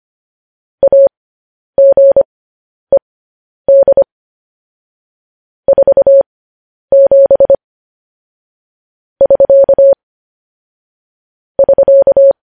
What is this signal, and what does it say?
Morse code, heard as a keyed tone, about 25 wpm, spaced as for 10 wpm, 550 Hz. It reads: AGED 47 <SK> <SK>